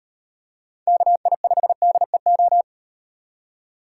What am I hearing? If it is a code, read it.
Morse code, heard as a keyed tone, about 38 wpm, 700 Hz. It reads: KI5DEO